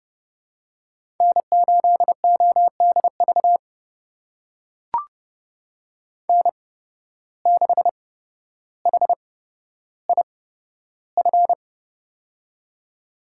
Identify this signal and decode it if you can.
Morse code, heard as a keyed tone, about 30 wpm, spaced as for 13 wpm, 700 Hz. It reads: N8ODV N6HIF
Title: USA Call Signs - 5 Characters - 8x Character Spacing - 30wpm